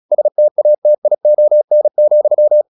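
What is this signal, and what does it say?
Morse code, heard as a keyed tone, about 36 wpm, 600 Hz. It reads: STATION,
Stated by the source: Morse practice at 36 wpm, text of Ken Follett's novel Never, from CW med Gnister.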